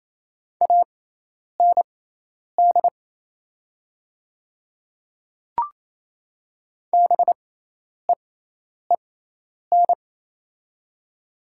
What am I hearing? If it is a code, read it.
Morse code, heard as a keyed tone, about 28 wpm, spaced as for 10 wpm, 700 Hz. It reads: AND BEEN
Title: 6x Character Spacing - Top 100 Words - 28wpm